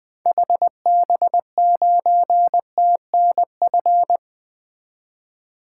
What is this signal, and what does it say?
Morse code, heard as a keyed tone, about 21 wpm, 700 Hz. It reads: HB9TNF
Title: 2023 SOTA Activator Call Signs - 20wpm